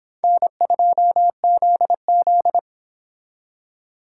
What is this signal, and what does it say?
Morse code, heard as a keyed tone, about 26 wpm, 700 Hz. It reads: N2ZZ